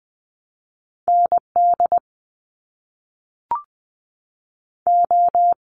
Morse code, heard as a keyed tone, about 20 wpm, spaced as for 5 wpm, 700 Hz. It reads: NDO
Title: US States 20wpm